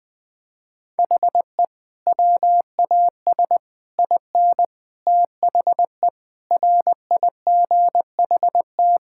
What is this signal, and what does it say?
Morse code, heard as a keyed tone, about 20 wpm, 700 Hz. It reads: HE WAS IN THE RIGHT
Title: Sentences with 7-Word Phrases - Spoken Beginning - 20wpm